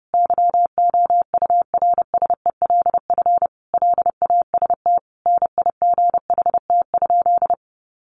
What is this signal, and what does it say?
Morse code, heard as a keyed tone, about 30 wpm, 700 Hz. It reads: YOURSELF LAST NIGHT?
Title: Sentences from Top 600 Words 30wpm